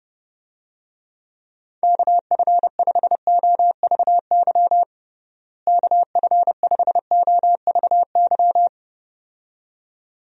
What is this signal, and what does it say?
Morse code, heard as a keyed tone, about 30 wpm, 700 Hz. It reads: KF5OVY KF5OVY
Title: USA Call Signs - 6 Characters - Repeated 2x - 30wpm